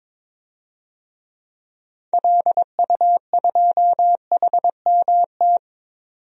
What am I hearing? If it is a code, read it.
Morse code, heard as a keyed tone, about 22 wpm, 700 Hz. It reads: LU2HMT
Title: DXCC-100 Call Signs - 6 Characters - 22wpm